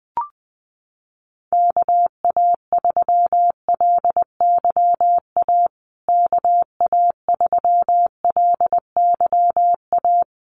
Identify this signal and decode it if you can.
Morse code, heard as a keyed tone, about 20 wpm, 700 Hz. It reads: KA3LYA KA3LYA
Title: USA Call Signs Repeated 3x 20wpm